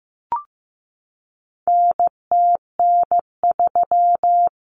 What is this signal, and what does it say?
Morse code, heard as a keyed tone, about 15 wpm, 700 Hz. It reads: NTN3
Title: Common Cut Numbers - 4-Digit 15wpm